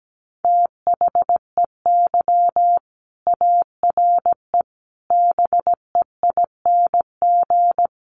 Morse code, heard as a keyed tone, about 17 wpm, 700 Hz. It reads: THEY ARE BEING
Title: Sentences from Top 600 Words 17wpm